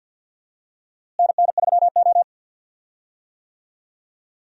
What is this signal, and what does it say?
Morse code, heard as a keyed tone, about 48 wpm, 700 Hz. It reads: NN3O
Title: USA Call Signs - 4 Characters - Repeated 3x - 50wpm